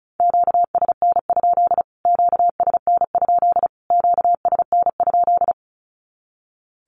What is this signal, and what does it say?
Morse code, heard as a keyed tone, about 34 wpm, 700 Hz. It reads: QSN? QSN? QSN?
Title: Q-Codes Repeated 3x 35wpm